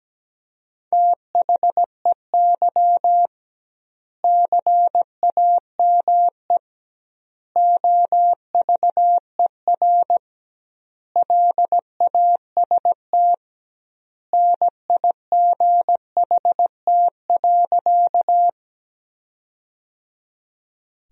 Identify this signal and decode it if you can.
Morse code, heard as a keyed tone, about 17 wpm, 700 Hz. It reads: THEY CAME OVER LAST NIGHT.
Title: 2x Word-Spacing - Sentences from Top 300 Words - 17wpm